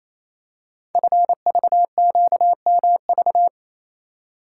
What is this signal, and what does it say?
Morse code, heard as a keyed tone, about 27 wpm, 690 Hz. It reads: FVQMV
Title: Code Groups - 5 Letter:Numbers - 28wpm